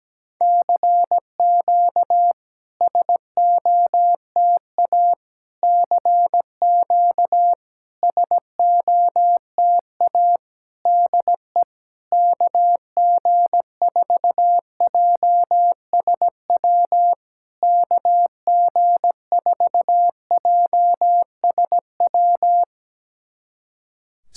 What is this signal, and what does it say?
Morse code, heard as a keyed tone, about 17 wpm, 700 Hz. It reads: CQ SOTA CQ SOTA DE KG4JSW KG4JSW
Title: SOTA QSO Practice 17wpm